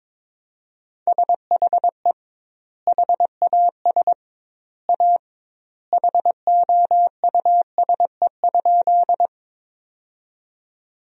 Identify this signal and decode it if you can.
Morse code, heard as a keyed tone, about 22 wpm, 700 Hz. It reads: SHE HAS A HOUSE?
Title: 2x Word-Spacing - Sentences from Top 200 Words - 22wpm